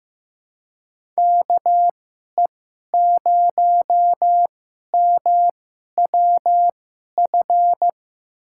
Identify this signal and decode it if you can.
Morse code, heard as a keyed tone, about 15 wpm, spaced as for 11 wpm, 700 Hz. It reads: KE0MWF
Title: USA Call Signs - 6 Characters - 2x Character Spacing - 15wpm